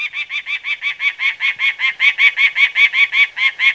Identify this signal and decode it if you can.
There is no code.